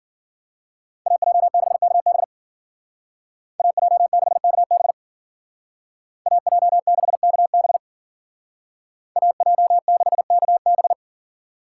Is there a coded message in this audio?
Yes: Morse code from 1.1 to 10.9 s, heard as a keyed tone, about 50 wpm, 700 Hz. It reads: AJ6KB AJ6KB AJ6KB AJ6KB